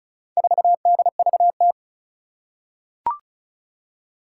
Morse code, heard as a keyed tone, about 36 wpm, 700 Hz. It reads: 4DVT
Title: Four Challenging Letter-Number Combinations 35wpm